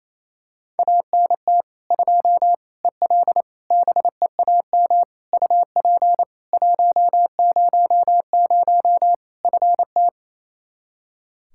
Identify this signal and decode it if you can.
Morse code, heard as a keyed tone, about 27 wpm, 700 Hz. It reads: ANT 2 EL BEAM UP 100 FT
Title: K1USN SST QSO Practice 28wpm